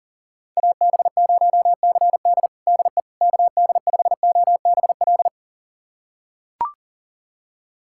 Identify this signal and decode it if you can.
Morse code, heard as a keyed tone, about 40 wpm, 700 Hz. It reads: AB0CD DE KD5OBL
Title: USA Callsign DE Callsign 40wpm